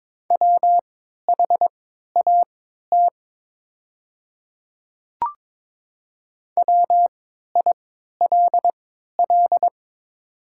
Morse code, heard as a keyed tone, about 22 wpm, spaced as for 13 wpm, 700 Hz. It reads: WHAT WILL
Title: 3x Character Spacing - Top 100 Words - 22wpm